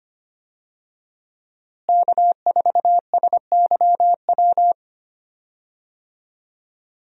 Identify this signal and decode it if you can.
Morse code, heard as a keyed tone, about 25 wpm, 700 Hz. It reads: K4SYW
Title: USA Call Signs - 5 Characters - Repeated 3x - 25wpm